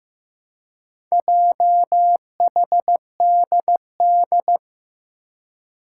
Morse code, heard as a keyed tone, about 15 wpm, 700 Hz. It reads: JHDD
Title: Four Challenging Letter-Number Combinations 15wpm